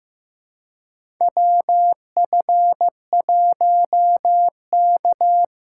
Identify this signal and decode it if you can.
Morse code, heard as a keyed tone, about 15 wpm, 700 Hz. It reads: WF1K
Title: USA Call Signs - 4 Characters - 3x Character Spacing - 15wpm